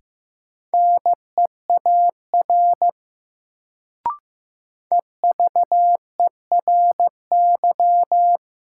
Morse code, heard as a keyed tone, about 15 wpm, 700 Hz. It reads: NEAR EVERY
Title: Sentences from Top 200 Words - New Word Drill - 15wpm